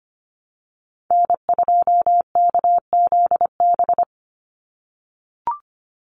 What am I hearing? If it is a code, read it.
Morse code, heard as a keyed tone, about 25 wpm, 690 Hz. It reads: N2KZB
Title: USA Call Signs 25wpm